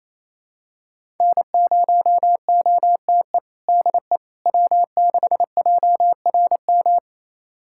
Morse code, heard as a keyed tone, about 28 wpm, 700 Hz. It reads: N0OTE DE W6JRM
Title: USA Callsign DE Callsign - 5 Characters - 28wpm